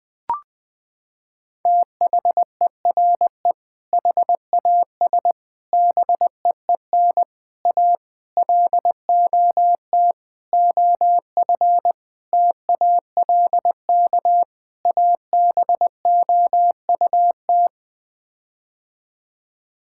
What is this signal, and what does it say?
Morse code, heard as a keyed tone, about 21 wpm, 700 Hz. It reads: THERE HAS BEEN A LOT OF TALK ABOUT
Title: Sets of 8 Words - 20wpm